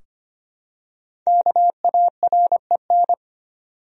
Morse code, heard as a keyed tone, about 25 wpm, 700 Hz. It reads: KAREN
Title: USA Common Names - 25wpm